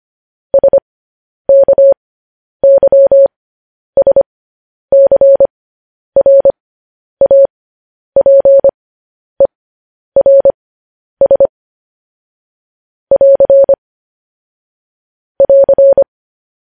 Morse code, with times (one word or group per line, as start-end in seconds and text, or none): 0.5-11.5 SKYSCRAPERS
13.1-13.7 <AR>
15.4-16.0 <AR>